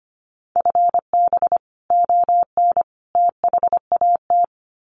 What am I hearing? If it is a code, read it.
Morse code, heard as a keyed tone, about 25 wpm, 700 Hz. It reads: FB ON THAT
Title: Common QSO Phrases 25wpm